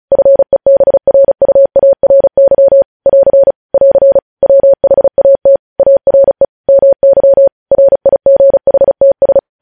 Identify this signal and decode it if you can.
Morse code, heard as a keyed tone, about 36 wpm, 550 Hz. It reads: FEBRUARY <AR> <AR> WHAT ARE MY RIGHTS
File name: podcast / News headlines-1x-35x35